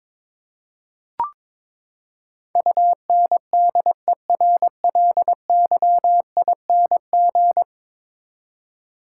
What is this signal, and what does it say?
Morse code, heard as a keyed tone, about 22 wpm, 700 Hz. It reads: UNDERLYING
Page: morse-code-ninja-practice-22wpm